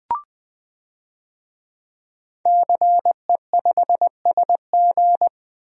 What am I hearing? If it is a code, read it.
Morse code, heard as a keyed tone, about 20 wpm, 700 Hz. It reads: CE5SG
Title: DXCC-100 Call Signs - 5 Characters - 20wpm